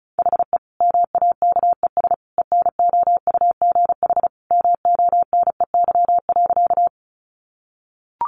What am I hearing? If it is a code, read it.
Morse code, heard as a keyed tone, about 34 wpm, 700 Hz. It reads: HE MAKES ENOUGH MONEY.
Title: Additional Easy Phrases 35wpm